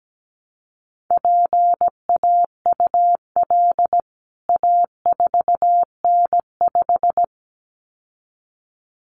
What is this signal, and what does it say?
Morse code, heard as a keyed tone, about 17 wpm, 700 Hz. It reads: PAUL A4N5
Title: CWT Exchange 17wpm